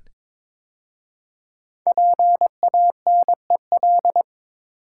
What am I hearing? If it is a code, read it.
Morse code, heard as a keyed tone, about 22 wpm, 700 Hz. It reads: PANEL
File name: 5 Letter Words - First 3 Letters Given - 22wpm